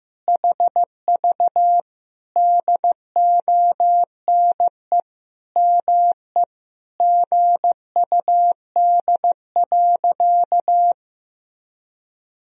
Morse code, heard as a keyed tone, about 15 wpm, 700 Hz. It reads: HV DONE ME GUD.
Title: Sentences from Top 500 Words and Abbreviations 15wpm